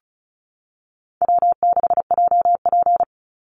Morse code, heard as a keyed tone, about 34 wpm, 700 Hz. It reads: W6JP
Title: NA SOTA Chaser Callsigns 35wpm